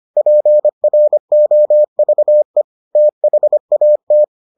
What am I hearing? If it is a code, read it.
Morse code, heard as a keyed tone, about 25 wpm, 600 Hz. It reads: PROVE THAT